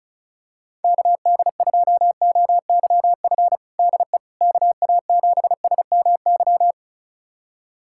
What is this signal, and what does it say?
Morse code, heard as a keyed tone, about 34 wpm, 700 Hz. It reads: KD2OYF DE KA7SMY
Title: USA Callsign DE Callsign - 6 Characters - 35wpm